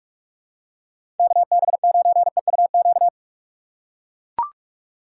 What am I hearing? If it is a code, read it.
Morse code, heard as a keyed tone, about 41 wpm, 700 Hz. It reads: KB9EUQ
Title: USA Call Signs - 2x Character Spacing - 45wpm